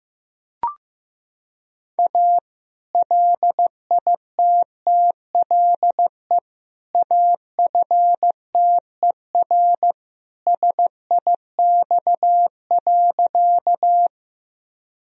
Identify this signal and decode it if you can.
Morse code, heard as a keyed tone, about 15 wpm, 700 Hz. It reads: A LITTLE AFTER SIX.
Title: Sentences from Top 1000 Words - Review Part 2 - 15wpm